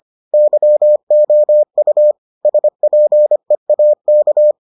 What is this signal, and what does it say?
Morse code, heard as a keyed tone, about 25 wpm, 600 Hz. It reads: YOU SPEAK